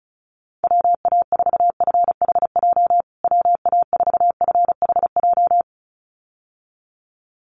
Morse code, heard as a keyed tone, about 34 wpm, 700 Hz. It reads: WA4FHJ WA4FHJ
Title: USA Call Signs Repeated 2x 35wpm